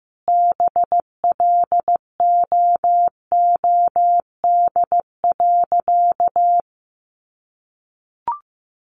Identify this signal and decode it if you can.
Morse code, heard as a keyed tone, about 15 wpm, 700 Hz. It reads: BLOOD.
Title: A Sound Of Thunder 15wpm